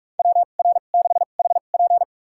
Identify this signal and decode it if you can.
Morse code, heard as a keyed tone, about 47 wpm, spaced as for 33 wpm, 700 Hz. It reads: WR6HP
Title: USA Call Signs - 2x Character Spacing - 45wpm